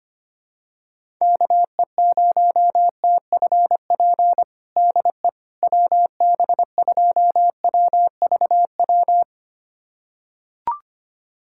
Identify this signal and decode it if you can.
Morse code, heard as a keyed tone, about 25 wpm, 700 Hz. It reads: KE0TFP DE WB2WVW
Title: USA Callsign DE Callsign - 6 Characters - 25wpm